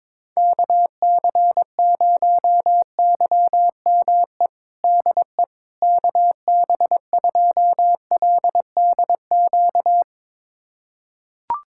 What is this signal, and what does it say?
Morse code, heard as a keyed tone, about 22 wpm, 700 Hz. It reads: KC0YME DE KB2LDQ
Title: USA Callsign DE Callsign - 6 Characters - 22wpm